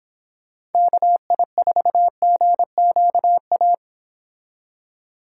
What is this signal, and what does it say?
Morse code, heard as a keyed tone, about 26 wpm, 700 Hz. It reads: KI4GQA
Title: Speed Racing - USA 2x3 Call Signs 22wpm